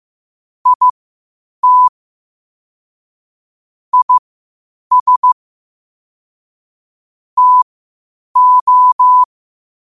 Morse code, heard as a keyed tone, about 15 wpm, spaced as for 8 wpm, 1000 Hz. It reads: IT IS TO
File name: qotdfiles / 8wpm